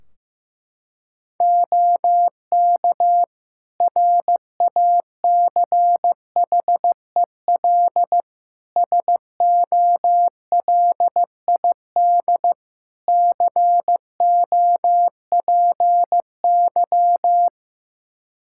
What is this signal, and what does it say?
Morse code, heard as a keyed tone, about 15 wpm, 700 Hz. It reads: OK RACHEL SOLID COPY
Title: K1USN SST QSO Practice 15wpm